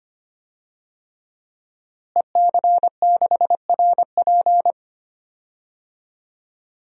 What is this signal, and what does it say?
Morse code, heard as a keyed tone, about 25 wpm, 700 Hz. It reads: EC6RP
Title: DXCC-100 Call Signs - 5 Characters - Repeated 3x - 25wpm